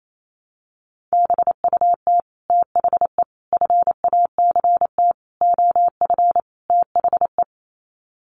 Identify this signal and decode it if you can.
Morse code, heard as a keyed tone, about 28 wpm, 700 Hz. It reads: BUT THE FACT OF THE
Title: Sets of 5 Words Encore2 28wpm